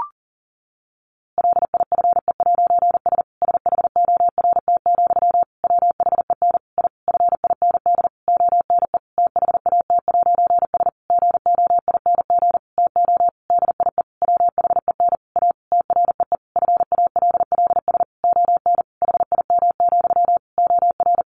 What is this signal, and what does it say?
Morse code, heard as a keyed tone, about 40 wpm, 700 Hz. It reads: LIFE'S SHORT, WHEN I FIND ONE THAT'S GOING TO DIE WHEN A TREE FALLS ON HIM, OR
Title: A Sound Of Thunder 40wpm